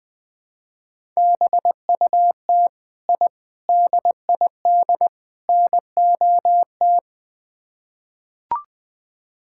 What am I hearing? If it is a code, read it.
Morse code, heard as a keyed tone, about 21 wpm, 700 Hz. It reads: BUT I DID NOT
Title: Build A Phrase - 4-Word Phrases - 20wpm